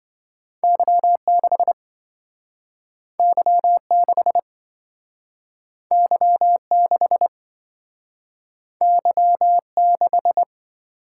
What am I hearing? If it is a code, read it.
Morse code, heard as a keyed tone, about 27 wpm, 700 Hz. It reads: Y6 Y6 Y6 Y6